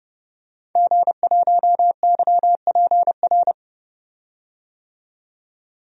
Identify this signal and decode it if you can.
Morse code, heard as a keyed tone, about 32 wpm, 700 Hz. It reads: G1YPR
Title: DXCC-100 Call Signs - 5 Characters - 30wpm